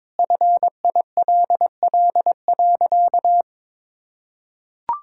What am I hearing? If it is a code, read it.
Morse code, heard as a keyed tone, about 22 wpm, 700 Hz. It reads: FILL.
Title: Sentences from Top 1000 Words - Review Part 5 - 22wpm